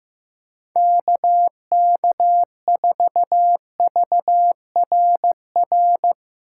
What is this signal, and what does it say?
Morse code, heard as a keyed tone, about 15 wpm, 700 Hz. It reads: KK4VRR
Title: USA Call Signs - 6 Characters - 4x Character Spacing - 15wpm